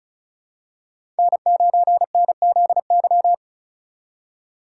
Morse code, heard as a keyed tone, about 36 wpm, 700 Hz. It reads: N9NZY
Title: USA Call Signs - 5 Characters - 4x Character Spacing - 35wpm